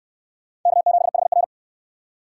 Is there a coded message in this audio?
Yes: Morse code from 0.7 to 1.4 s, heard as a keyed tone, about 66 wpm, 700 Hz. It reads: K7HR